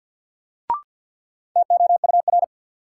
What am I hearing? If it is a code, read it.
Morse code, heard as a keyed tone, about 48 wpm, 700 Hz. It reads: TOUR
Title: Sentences from Top 1400 Words - New Word Drill - 50wpm